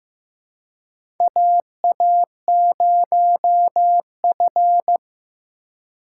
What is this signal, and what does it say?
Morse code, heard as a keyed tone, about 15 wpm, 700 Hz. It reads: AA0F
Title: USA Call Signs - 4 Characters - 3x Character Spacing - 15wpm